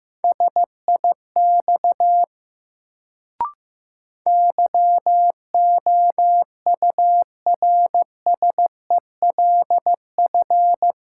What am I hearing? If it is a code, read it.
Morse code, heard as a keyed tone, about 15 wpm, 700 Hz. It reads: SIX YOURSELF
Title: Sentences from Top 600 Words - New Word Drill - 15wpm